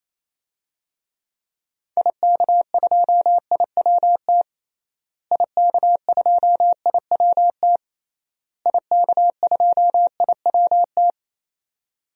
Morse code, heard as a keyed tone, about 27 wpm, 700 Hz. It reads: IK2IWT IK2IWT IK2IWT
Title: DXCC-100 Call Signs - 6 Characters - Repeated 3x - 28wpm